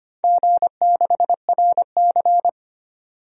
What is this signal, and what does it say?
Morse code, heard as a keyed tone, about 25 wpm, 700 Hz. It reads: G6RC
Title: DXCC-100 Call Signs - 4 Characters - 25wpm